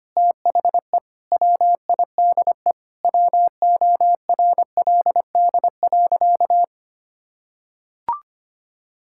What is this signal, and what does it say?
Morse code, heard as a keyed tone, about 25 wpm, 700 Hz. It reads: THE WIDE WORLD.
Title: Sentences from Top 1000 Words - Review Part 6 - 25wpm